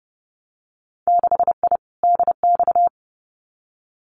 Morse code, heard as a keyed tone, about 30 wpm, 700 Hz. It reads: 6I DX
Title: Winter Field Day Exchanges Encore 30wpm